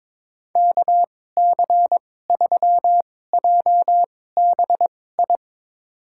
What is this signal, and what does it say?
Morse code, heard as a keyed tone, about 22 wpm, spaced as for 16 wpm, 700 Hz. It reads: KC3JBI
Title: USA Call Signs - 6 Characters - 2x Character Spacing - 22wpm